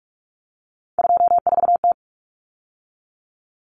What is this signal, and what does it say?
Morse code, heard as a keyed tone, about 44 wpm, 700 Hz. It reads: 24T
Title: Common Cut Numbers - 3-Digit 45wpm